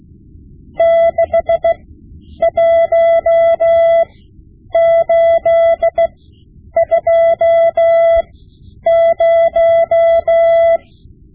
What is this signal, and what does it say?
Morse code, heard as a keyed tone, about 16 wpm, spaced as for 9 wpm, 670 Hz. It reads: 61820
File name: Morse Lessons / Morse Lessons / Tape-5B